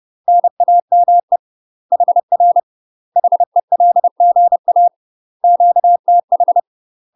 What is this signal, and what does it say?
Morse code, heard as a keyed tone, about 30 wpm, 700 Hz. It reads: NAME HR HELGA QTH
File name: UK MASTER QSOs TRAINING / 30WPM / UK MASTER QSOs 0048